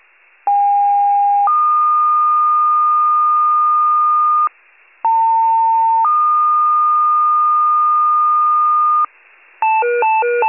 Two-tone page, heard as a one-second tone, then a three-second tone, about 810 then 1220 Hz, about 900 then 1220 Hz. It is heard from 0.5 to 9.1 s.
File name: wcalert